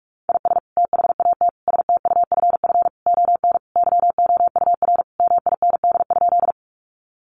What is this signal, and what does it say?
Morse code, heard as a keyed tone, about 44 wpm, 700 Hz. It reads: IS THAT STUFF ON YOUR MIND?
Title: Sentences from Top 1400 Words 45wpm